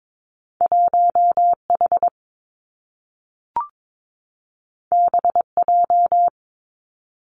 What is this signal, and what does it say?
Morse code, heard as a keyed tone, about 22 wpm, 700 Hz. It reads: 1 H B J
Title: Two Challenging Letter-Number Combinations 22wpm